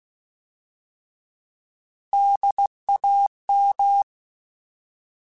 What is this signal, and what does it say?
Morse code, heard as a keyed tone, about 16 wpm, 800 Hz. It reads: DAM
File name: Alphabetical / Alphabetical mp3s / Ds @15